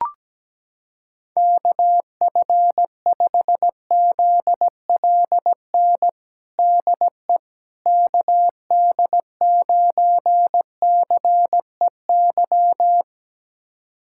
Morse code, heard as a keyed tone, about 17 wpm, 700 Hz. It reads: KF5ZLN DE KD9CEY
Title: USA Callsign DE Callsign 17wpm